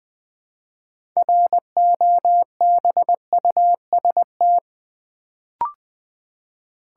Morse code, heard as a keyed tone, about 20 wpm, 700 Hz. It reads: ROBUST
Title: Build A Word - 3 Letters At A Time - 6-Letter Words - 20wpm